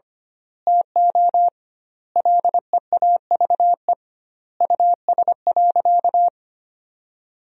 Morse code, heard as a keyed tone, about 25 wpm, 700 Hz. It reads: TO LEAVE US.
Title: 2x Word-Spacing - Sentences from Top 400 Words - 25wpm